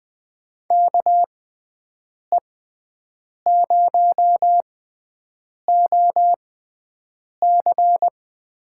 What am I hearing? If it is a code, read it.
Morse code, heard as a keyed tone, about 20 wpm, spaced as for 7 wpm, 700 Hz. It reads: KE0OC
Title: USA Call Signs - 6 Characters - 6x Character Spacing - 20wpm